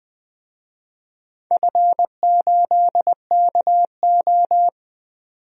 Morse code, heard as a keyed tone, about 20 wpm, 700 Hz. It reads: F8KO
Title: DXCC-100 Call Signs - 4 Characters - 20wpm